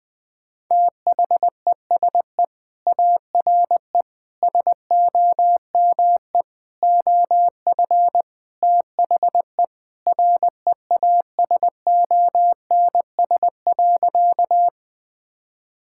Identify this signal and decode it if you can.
Morse code, heard as a keyed tone, about 20 wpm, 700 Hz. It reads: THESE ARE SOME OF THE REASONS.
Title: Sentences with 4-Word Phrases - 20wpm